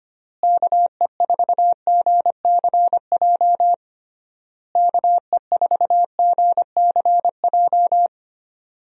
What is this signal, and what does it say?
Morse code, heard as a keyed tone, about 25 wpm, 700 Hz. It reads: KE4GCJ KE4GCJ
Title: USA Call Signs - 6 Characters - Repeated 3x - 25wpm